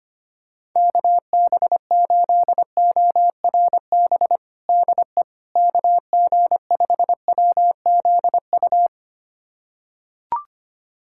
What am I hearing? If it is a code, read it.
Morse code, heard as a keyed tone, about 25 wpm, 700 Hz. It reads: KB8ORB DE KG5WZU
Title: USA Callsign DE Callsign - 6 Characters - 25wpm